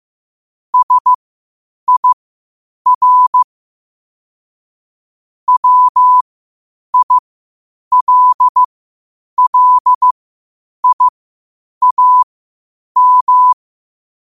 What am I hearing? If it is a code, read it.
Morse code, heard as a keyed tone, about 15 wpm, spaced as for 8 wpm, 1000 Hz. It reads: SIR WILLIAM
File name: qotdfiles / 8wpm